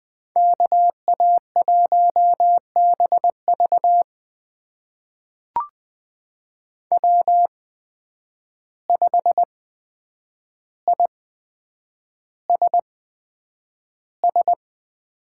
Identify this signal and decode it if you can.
Morse code, heard as a keyed tone, about 20 wpm, spaced as for 9 wpm, 700 Hz. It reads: KA1BV W5ISS
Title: USA Call Signs - 8x Character Spacing - 20wpm